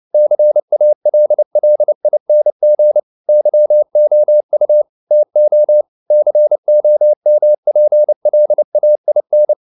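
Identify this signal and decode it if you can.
Morse code, heard as a keyed tone, about 29 wpm, 600 Hz. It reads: CALLING YOU TO COMPLAIN
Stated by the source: Morse code practice at 28 wpm, text of Ken Follett's novel Never in English, from CW med Gnister.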